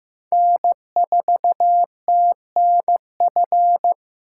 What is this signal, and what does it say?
Morse code, heard as a keyed tone, about 16 wpm, 700 Hz. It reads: N4TNF
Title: USA Call Signs - 5 Characters - 2x Character Spacing - 15wpm